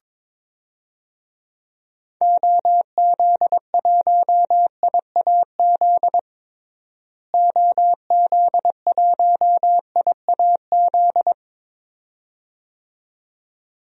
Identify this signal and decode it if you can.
Morse code, heard as a keyed tone, about 22 wpm, 700 Hz. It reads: OZ1IAZ OZ1IAZ
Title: DXCC-100 Call Signs - 6 Characters - Repeated 2x - 22wpm